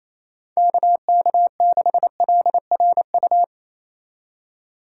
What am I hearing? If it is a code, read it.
Morse code, heard as a keyed tone, about 27 wpm, 700 Hz. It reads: KK6LRU
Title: USA Call Signs - 6 Characters - 4x Character Spacing - 28wpm